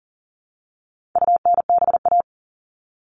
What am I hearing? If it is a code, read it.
Morse code, heard as a keyed tone, about 38 wpm, 700 Hz. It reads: UNBA
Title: All Cut Numbers - 4-Digit 40wpm